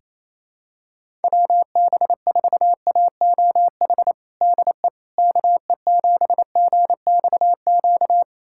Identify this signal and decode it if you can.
Morse code, heard as a keyed tone, about 28 wpm, 700 Hz. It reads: WB4AOH DE KE7GXQ
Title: USA Callsign DE Callsign 28wpm